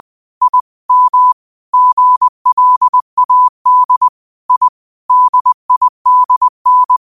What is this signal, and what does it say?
Morse code, heard as a keyed tone, about 20 wpm, 1000 Hz. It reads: I M GLAD I DIDN
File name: qotdfiles / 20wpm